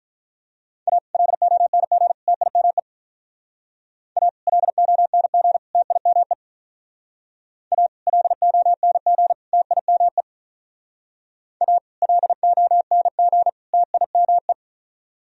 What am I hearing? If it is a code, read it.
Morse code, heard as a keyed tone, about 43 wpm, 700 Hz. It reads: A LONG TIME A LONG TIME A LONG TIME A LONG TIME
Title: Speed-Racing - Sets of 3 Words 35wpm